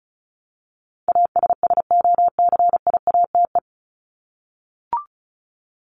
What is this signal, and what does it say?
Morse code, heard as a keyed tone, about 34 wpm, 700 Hz. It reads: ASSOCIATE
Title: Top 1751-2000 Words 35wpm